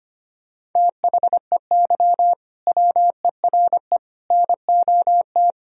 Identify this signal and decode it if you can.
Morse code, heard as a keyed tone, about 25 wpm, 700 Hz. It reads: THEY WERE NOT
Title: Sentences from Top 1000 Words - Review Part 3 - 25wpm